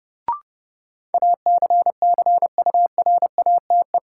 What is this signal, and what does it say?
Morse code, heard as a keyed tone, about 30 wpm, 700 Hz. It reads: ACCURATE
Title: Sentences from Top 1200 Words - New Word Drill Alt - 30wpm